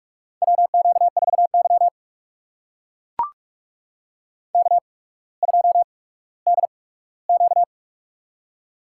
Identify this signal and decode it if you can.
Morse code, heard as a keyed tone, about 45 wpm, spaced as for 19 wpm, 700 Hz. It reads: WQ4Y K2DQ
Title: USA Call Signs - 4 Characters - 8x Character Spacing - 45wpm